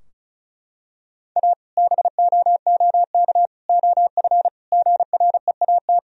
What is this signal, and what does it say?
Morse code, heard as a keyed tone, about 36 wpm, 700 Hz. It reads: A BOOK OF GREAT
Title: Sentences from Top 1000 Words - Review Part 2 - 35wpm